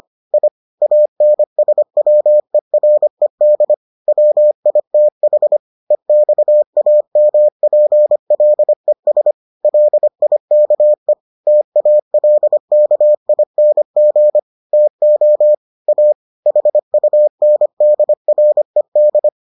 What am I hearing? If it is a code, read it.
Morse code, heard as a keyed tone, about 25 wpm, 600 Hz. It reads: I ANSWERED WITH EXAMPLES LIKE TALKING TO A HUNDRED